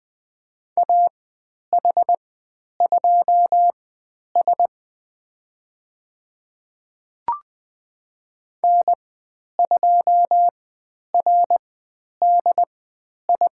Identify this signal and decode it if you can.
Morse code, heard as a keyed tone, about 20 wpm, spaced as for 10 wpm, 700 Hz. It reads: AH2S N2RDI
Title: 095 Lesson 16g - Letter c - Call Signs 20wpm